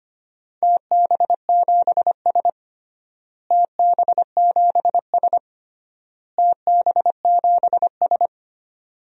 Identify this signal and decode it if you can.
Morse code, heard as a keyed tone, about 25 wpm, 700 Hz. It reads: TB7S TB7S TB7S
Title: DXCC-100 Call Signs - 4 Characters - Repeated 3x - 25wpm